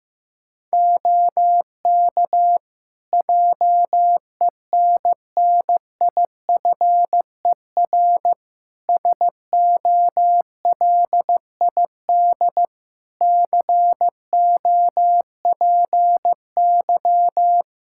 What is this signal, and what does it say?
Morse code, heard as a keyed tone, about 16 wpm, 700 Hz. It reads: OK JENNIFER SOLID COPY
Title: Practice QSO - Exchange 3 - More Info - 15wpm